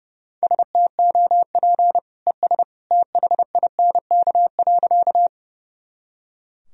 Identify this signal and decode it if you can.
Morse code, heard as a keyed tone, about 30 wpm, 700 Hz. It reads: STOP ES THINK.
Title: Sentences from Top 300 Words and Abbreviations 30wpm